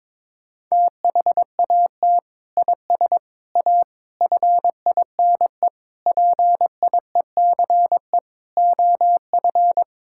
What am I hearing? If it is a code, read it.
Morse code, heard as a keyed tone, about 22 wpm, 700 Hz. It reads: THAT IS A FINE PIECE OF
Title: Sentences from Top 1000 Words - Review Part 5 - 22wpm